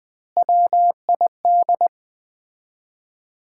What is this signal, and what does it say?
Morse code, heard as a keyed tone, about 20 wpm, 700 Hz. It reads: WID
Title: Common QSO Phrases - 4x Word-Spacing - 20wpm